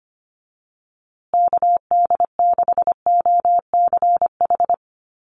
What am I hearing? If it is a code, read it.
Morse code, heard as a keyed tone, about 25 wpm, 700 Hz. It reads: KD6OCH
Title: USA Call Signs Repeated 3x 25wpm